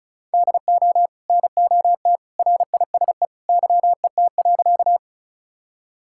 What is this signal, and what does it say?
Morse code, heard as a keyed tone, about 34 wpm, 700 Hz. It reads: DO NOT RISE YET.
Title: Sentences from Top 1000 Words - Review Part 6 - 35wpm